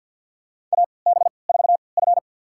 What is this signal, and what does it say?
Morse code, heard as a keyed tone, about 46 wpm, spaced as for 27 wpm, 700 Hz. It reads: AB4F
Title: USA Call Signs - 4 Characters - 3x Character Spacing - 50wpm